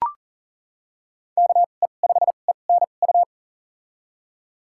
Morse code, heard as a keyed tone, about 40 wpm, spaced as for 29 wpm, 700 Hz. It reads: KE5ENU
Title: USA Call Signs - 2x Character Spacing - 40wpm